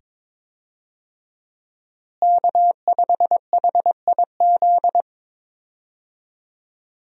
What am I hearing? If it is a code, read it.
Morse code, heard as a keyed tone, about 22 wpm, 700 Hz. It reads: K5HIZ
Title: USA Call Signs - 5 Characters - Repeated 2x - 22wpm